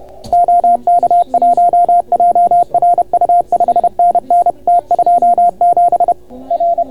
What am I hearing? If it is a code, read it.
Morse code, heard as a keyed tone, about 31 wpm, 690 Hz. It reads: OK1JRU 5NN T27 R